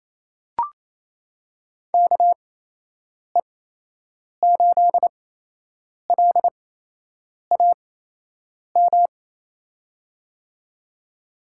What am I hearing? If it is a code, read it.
Morse code, heard as a keyed tone, about 28 wpm, spaced as for 8 wpm, 700 Hz. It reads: KE8LAM
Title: USA Call Signs - 6 Characters - 8x Character Spacing - 28wpm